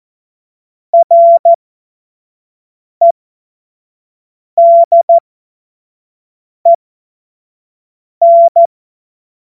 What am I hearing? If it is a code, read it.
Morse code, heard as a keyed tone, about 14 wpm, spaced as for 5 wpm, 680 Hz. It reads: REDEN
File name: Morsekurs / Witze / Witz 18. April 2023 25 F70